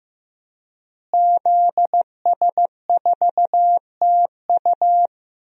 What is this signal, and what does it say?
Morse code, heard as a keyed tone, about 15 wpm, 700 Hz. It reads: ZS4TU
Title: DXCC-100 Call Signs - 5 Characters - Repeated 3x - 15wpm